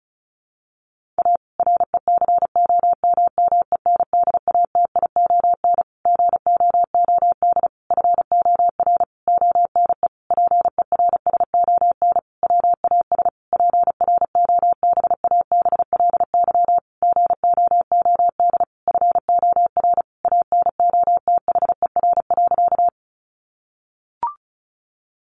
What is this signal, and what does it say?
Morse code, heard as a keyed tone, about 34 wpm, 700 Hz. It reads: A RECOMMENDATION GOOD FOR ONE PERSON WAS PROBABLY GOOD FOR ANOTHER.
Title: Sentences from Top 1400 Words 35wpm